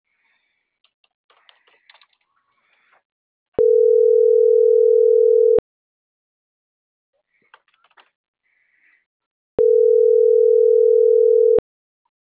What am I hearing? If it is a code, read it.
Ringback tone (North America), 440 + 480 Hz.